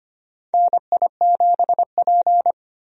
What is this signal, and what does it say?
Morse code, heard as a keyed tone, about 25 wpm, 700 Hz. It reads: NI7P
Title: USA Call Signs - 4 Characters - 25wpm